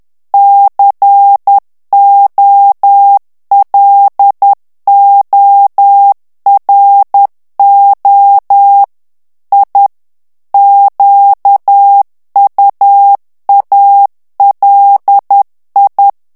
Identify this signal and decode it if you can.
Morse code, heard as a keyed tone, about 10 wpm, 800 Hz. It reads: COLORO I QUALI
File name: ITALIEN texte clair 2 vitesse 10 mots minute